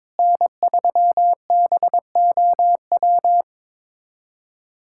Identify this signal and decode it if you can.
Morse code, heard as a keyed tone, about 22 wpm, 700 Hz. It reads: N3BOW
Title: USA Callsign DE Callsign 22wpm